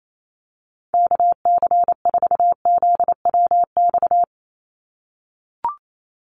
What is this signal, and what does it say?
Morse code, heard as a keyed tone, about 28 wpm, 700 Hz. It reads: KC4ZWX
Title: USA Call Signs 28wpm